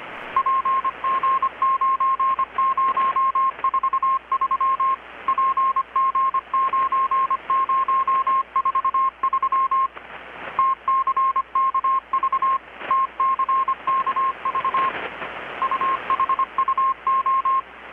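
Morse code, heard as a keyed tone, about 25 wpm, 1070 Hz. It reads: PG9043 PG9043 TCKV TCKV UHUO
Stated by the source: Morse from KPH's Night of Nights XX maritime radiotelegraphy event, received in Chiapas, Mexico about 3,800 km away on a Kenwood R-600 with a grounded 10 m long-wire antenna.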